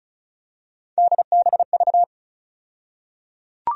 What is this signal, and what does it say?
Morse code, heard as a keyed tone, about 36 wpm, 700 Hz. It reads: DBV
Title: USA Call Sign Postfixes - 3 Letters 35wpm